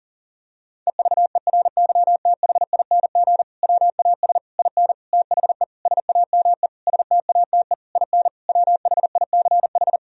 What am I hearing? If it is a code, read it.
Morse code, heard as a keyed tone, about 40 wpm, 700 Hz. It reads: EVERYTHING WAS IN THE SAME STATE IN WHICH
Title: Sentences from Top 1000 Words - Review Part 2 - 40wpm